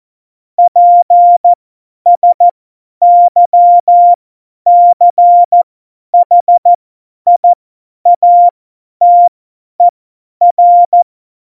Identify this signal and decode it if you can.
Morse code, heard as a keyed tone, about 14 wpm, spaced as for 10 wpm, 700 Hz. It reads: PSYCHIATER
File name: Morsekurs / Witze / Witz 20 March 2018, 50 F70